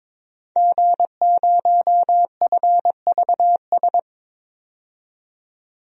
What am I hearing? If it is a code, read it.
Morse code, heard as a keyed tone, about 22 wpm, 700 Hz. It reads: G0FVS
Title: DXCC-100 Call Signs - 5 Characters - 22wpm